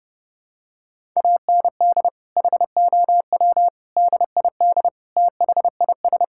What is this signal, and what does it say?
Morse code, heard as a keyed tone, about 28 wpm, 700 Hz. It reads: AND HOW DID THIS
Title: Sentences from Top 1000 Words - Review Part 6 - 30wpm